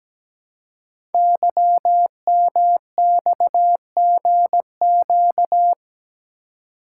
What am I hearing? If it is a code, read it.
Morse code, heard as a keyed tone, about 17 wpm, 700 Hz. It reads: YMXGQ
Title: Code Groups - 5 Letters - 17wpm